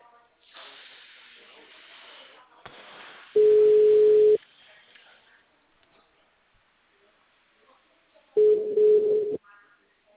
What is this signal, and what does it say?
Ringback tone (Europe, 425 Hz).
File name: out- -2002- - - .24507